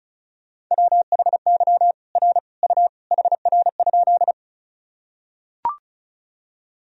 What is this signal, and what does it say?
Morse code, heard as a keyed tone, about 34 wpm, 690 Hz. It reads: WHY R U HR?
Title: Sentences from Top 200 Words and Abbreviations 35wpm